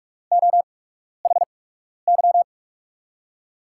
Morse code, heard as a keyed tone, about 44 wpm, spaced as for 12 wpm, 700 Hz. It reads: OHY
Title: USA Call Signs - 6 Characters - 8x Character Spacing - 45wpm